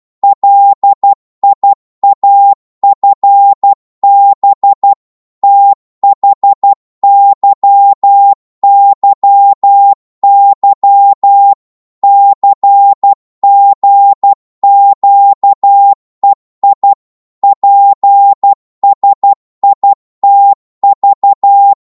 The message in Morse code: LIAFB THYYY CGQEI PSITV